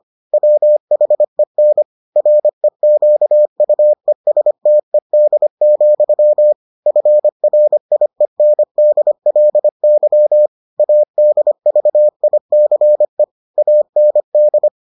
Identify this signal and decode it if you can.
Morse code, heard as a keyed tone, about 25 wpm, 600 Hz. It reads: WHEN REQUESTED, FRIENDLY ADVICE AND